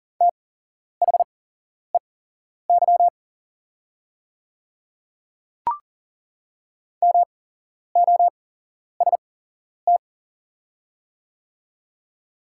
Morse code, heard as a keyed tone, about 40 wpm, spaced as for 11 wpm, 690 Hz. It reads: THEY MOST
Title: 8x Character Spacing - Top 100 Words - 40wpm